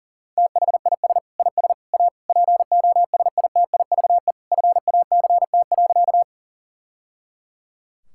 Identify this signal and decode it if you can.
Morse code, heard as a keyed tone, about 40 wpm, 700 Hz. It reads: THIS IS A POSITIVE FACT.
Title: Sentences from Top 1000 Words - Review Part 5 - 40wpm